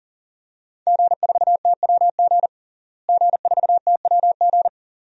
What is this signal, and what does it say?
Morse code, heard as a keyed tone, about 40 wpm, 700 Hz. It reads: G4TWG G4TWG
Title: DXCC-100 Call Signs - 5 Characters - Repeated 2x - 40wpm